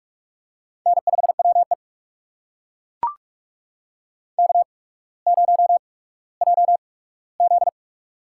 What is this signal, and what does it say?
Morse code, heard as a keyed tone, about 43 wpm, spaced as for 19 wpm, 700 Hz. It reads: N5WE K0JZ
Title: USA Call Signs - 4 Characters - 8x Character Spacing - 45wpm